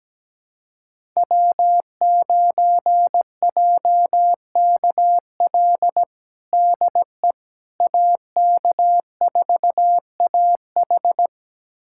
Morse code, heard as a keyed tone, about 17 wpm, 700 Hz. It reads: W9JKL DE AK4AH
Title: USA Callsign DE Callsign 17wpm